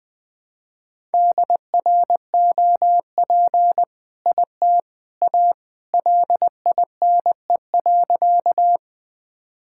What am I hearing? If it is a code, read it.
Morse code, heard as a keyed tone, about 20 wpm, 700 Hz. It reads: DROP IT A LINE.
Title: All-Age Appropriate Jokes - 20wpm